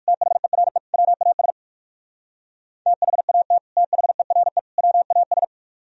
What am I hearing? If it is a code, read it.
Morse code, heard as a keyed tone, about 58 wpm, 700 Hz. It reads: THERE WAS THAT THERE WAS